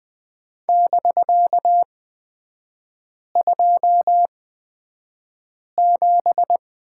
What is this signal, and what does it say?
Morse code, heard as a keyed tone, about 20 wpm, spaced as for 5 wpm, 700 Hz. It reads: <BK>27
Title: 250 Lesson 40e - BK - Three Characters 20wpm